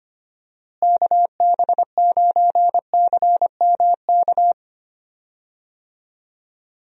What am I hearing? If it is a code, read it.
Morse code, heard as a keyed tone, about 25 wpm, 700 Hz. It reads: KB9CMK
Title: USA Call Signs - 6 Characters - Repeated 2x - 25wpm